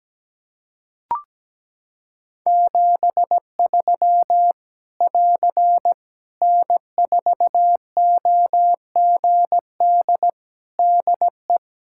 Morse code, heard as a keyed tone, about 17 wpm, 700 Hz. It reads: 73 <AR> N4OGD DE
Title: K1USN SST QSO Practice 17wpm